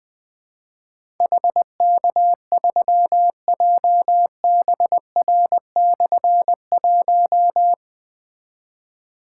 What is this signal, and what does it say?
Morse code, heard as a keyed tone, about 21 wpm, 700 Hz. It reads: HK3JBR/1
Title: Difficult Call Signs 20wpm